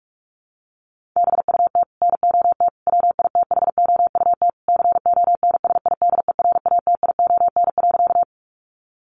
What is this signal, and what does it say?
Morse code, heard as a keyed tone, about 44 wpm, 700 Hz. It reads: BUT NOT WITHOUT CONSIDERATION.
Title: Sentences from Top 1500 Words 45wpm